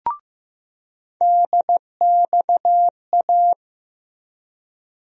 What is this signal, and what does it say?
Morse code, heard as a keyed tone, about 15 wpm, 700 Hz. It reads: DXA
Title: USA Call Sign Postfixes - 3 Letters 15wpm